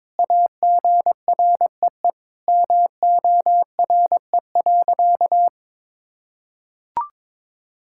Morse code, heard as a keyed tone, about 22 wpm, 700 Hz. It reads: AGREE MORE.
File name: practice / 22wpm / Sentences with 3-Word Phrases - 22wpm